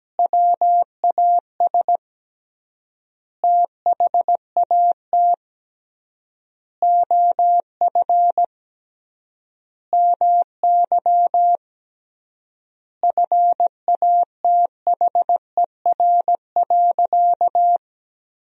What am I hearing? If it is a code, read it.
Morse code, heard as a keyed tone, about 17 wpm, 700 Hz. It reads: WAS THAT OF MY FATHER.
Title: 3x Word-Spacing - Sentences from Top 500 Words - 17wpm